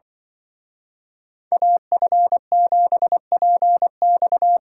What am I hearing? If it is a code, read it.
Morse code, heard as a keyed tone, about 24 wpm, 700 Hz. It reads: AF7PX
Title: Speed Racing - USA 2x2 Call Signs 20wpm